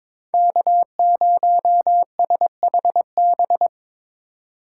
Morse code, heard as a keyed tone, about 25 wpm, 700 Hz. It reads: K0SHB
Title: Speed Racing - USA 1x3 Call Signs 22wpm